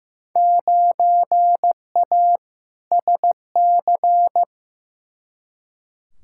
Morse code, heard as a keyed tone, about 15 wpm, 700 Hz. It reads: 9A SC
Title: Field Day Exchanges Encore - 15wpm